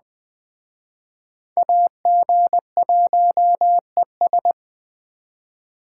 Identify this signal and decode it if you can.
Morse code, heard as a keyed tone, about 20 wpm, 700 Hz. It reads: AG1ES